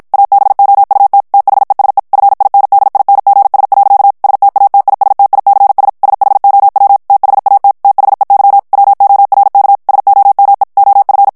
Morse code, heard as a keyed tone, about 56 wpm, 800 Hz. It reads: ABOUT THESE FINDINGS. STATISTICS SHOW THAT THEY WORK SOME OF
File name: morse mp3 / ANGLAIS texte clair 1 vitesse 50 mots minute